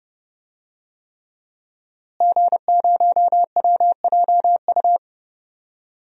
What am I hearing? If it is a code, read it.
Morse code, heard as a keyed tone, about 32 wpm, 700 Hz. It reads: G0WJU
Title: DXCC-100 Call Signs - 5 Characters - Repeated 3x - 30wpm